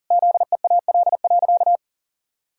Morse code, heard as a keyed tone, about 37 wpm, 700 Hz. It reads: ZEAL.
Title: Pangram Challenge v2 - 40wpm